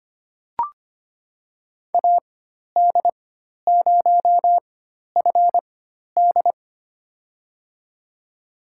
Morse code, heard as a keyed tone, about 25 wpm, spaced as for 12 wpm, 700 Hz. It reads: AD0FD
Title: USA Call Signs - 4x Character Spacing - 25wpm